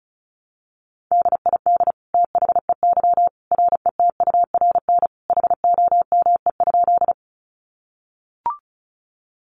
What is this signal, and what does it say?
Morse code, heard as a keyed tone, about 34 wpm, 700 Hz. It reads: DID THEY RETURN HOME?
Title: Sentences from Top 800 Words 35wpm